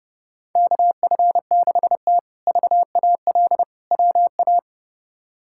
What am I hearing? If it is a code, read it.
Morse code, heard as a keyed tone, about 29 wpm, 700 Hz. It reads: KF6T VAL WA
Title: K1USN SST QSO Practice - 30wpm